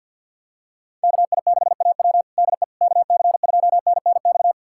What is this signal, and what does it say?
Morse code, heard as a keyed tone, about 50 wpm, 700 Hz. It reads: KI6AW DE KC2NNX
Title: USA Callsign DE Callsign 50wpm